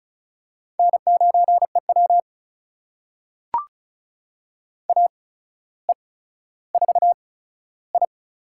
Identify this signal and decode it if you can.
Morse code, heard as a keyed tone, about 36 wpm, spaced as for 15 wpm, 700 Hz. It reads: N9EW AE4I
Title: USA Call Signs - 8x Character Spacing - 35wpm